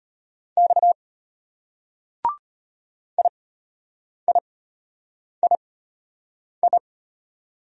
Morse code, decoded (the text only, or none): X IIII